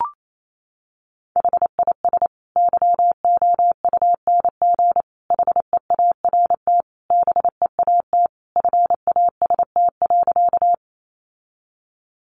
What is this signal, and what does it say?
Morse code, heard as a keyed tone, about 28 wpm, 700 Hz. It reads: HIS YOUNG HEART BEAT FAST.
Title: Sentences from Top 1300 Words 28wpm